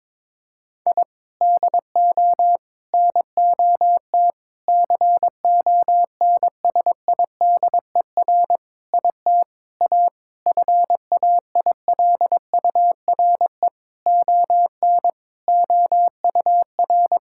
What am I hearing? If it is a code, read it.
Morse code, heard as a keyed tone, about 22 wpm, 700 Hz. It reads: I DO NOT CONSIDER IT A FAILURE ON OUR
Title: Sentences from Top 1000 Words - Review Part 6 - 22wpm